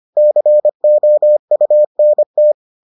Morse code, heard as a keyed tone, about 25 wpm, 600 Hz. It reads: COUNT